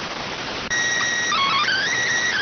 SSTV (slow-scan television), PD 290.